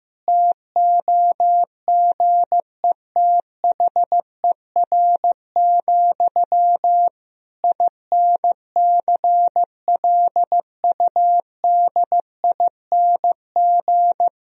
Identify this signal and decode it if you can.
Morse code, heard as a keyed tone, about 15 wpm, 700 Hz. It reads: TOGETHER, INCLUDING
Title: Sentences from Top 1000 Words - Review Part 4 - 15wpm